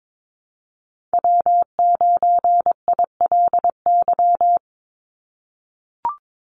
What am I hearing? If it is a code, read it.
Morse code, heard as a keyed tone, about 22 wpm, 700 Hz. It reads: W9ILY
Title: Call Signs - 22wpm